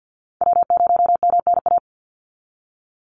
Morse code, heard as a keyed tone, about 50 wpm, 700 Hz. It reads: W0MNA
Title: NA SOTA Chaser Callsigns 50wpm